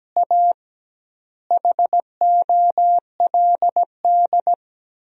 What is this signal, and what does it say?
Morse code, heard as a keyed tone, about 17 wpm, 700 Hz. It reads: A HOLD
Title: 2x Word-Spacing - Sentences from Top 400 Words - 17wpm